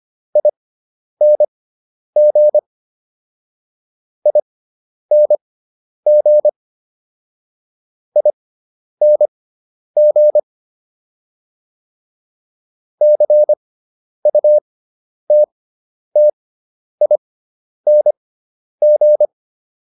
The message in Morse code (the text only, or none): ING ING ING CUTTING